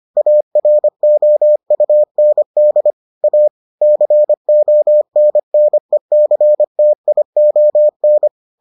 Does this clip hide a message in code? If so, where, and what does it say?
Yes: Morse code from 0.2 to 8.3 s, heard as a keyed tone, about 25 wpm, 600 Hz. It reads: AROUND A CONNECTION